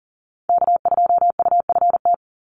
Morse code, heard as a keyed tone, about 40 wpm, 700 Hz. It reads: K2UFT
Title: Call Signs - 40wpm